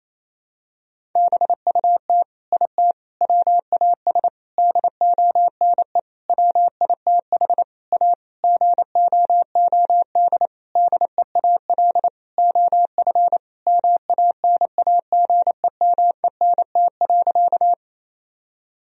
Morse code, heard as a keyed tone, about 28 wpm, 700 Hz. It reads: BUT IT WAS DONE WITH A GOOD DEAL OF MANAGEMENT.
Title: Sentences from Top 1000 Words - Review Part 4 - 28wpm